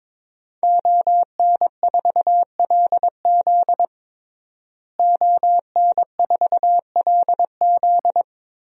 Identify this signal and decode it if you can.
Morse code, heard as a keyed tone, about 22 wpm, 700 Hz. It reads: ON4LZ ON4LZ
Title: DXCC-100 Call Signs - 5 Characters - Repeated 3x - 22wpm